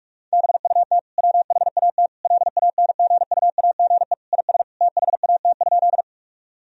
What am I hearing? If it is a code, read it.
Morse code, heard as a keyed tone, about 41 wpm, 700 Hz. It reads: BUT WHAT LANGUAGE IS THAT?
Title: Sentences from Top 1000 Words - Review Part 3 - 45wpm